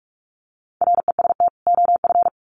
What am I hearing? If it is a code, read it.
Morse code, heard as a keyed tone, about 47 wpm, 700 Hz. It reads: REST OF
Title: Sets of 2 Words 45wpm